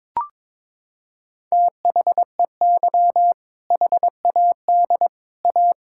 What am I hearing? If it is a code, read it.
Morse code, heard as a keyed tone, about 22 wpm, 700 Hz. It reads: THEY HAD A
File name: practice / 22wpm / Sentences with 2-Word Phrases - 22wpm